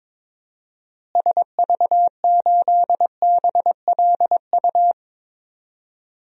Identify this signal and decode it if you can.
Morse code, heard as a keyed tone, about 22 wpm, 700 Hz. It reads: SV8BLU
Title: DXCC-100 Call Signs - 6 Characters - Repeated 2x - 22wpm